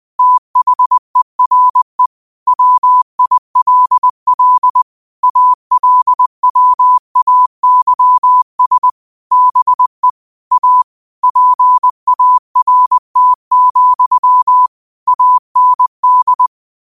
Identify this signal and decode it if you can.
Morse code, heard as a keyed tone, about 20 wpm, 1000 Hz. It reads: THERE WILL ALWAYS BE A PART, AND
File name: qotdfiles / 20wpm